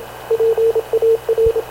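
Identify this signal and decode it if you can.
Morse code, heard as a keyed tone, about 27 wpm, 450 Hz, with radio noise. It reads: PAR
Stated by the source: Havana Radio / CLA sending Spanish press (PX) in Morse, recorded by Radio Officer Paul Dumesnil onboard CNAV QUEST/CZDO.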